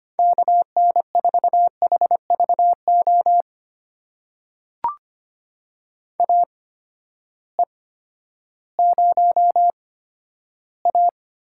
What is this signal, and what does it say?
Morse code, heard as a keyed tone, about 25 wpm, spaced as for 12 wpm, 700 Hz. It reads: KN4HVO AE0A
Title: USA Call Signs - 8x Character Spacing - 25wpm